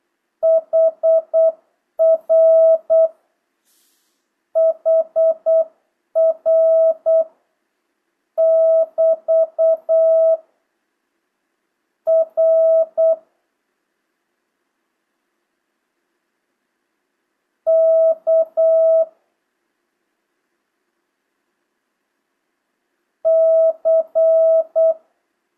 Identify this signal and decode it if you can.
Morse code, heard as a keyed tone, about 8 wpm, 650 Hz. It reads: HR HR <BT> R K C
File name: jh8cbh / 2025cw